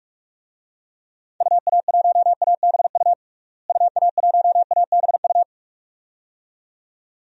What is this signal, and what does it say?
Morse code, heard as a keyed tone, about 44 wpm, 700 Hz. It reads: UA1ABU UA1ABU
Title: DXCC-100 Call Signs - 6 Characters - Repeated 2x - 45wpm